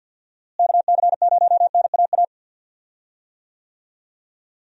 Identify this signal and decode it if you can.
Morse code, heard as a keyed tone, about 51 wpm, 700 Hz. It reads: KC0NAA
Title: USA Call Signs - 6 Characters - Repeated 3x - 50wpm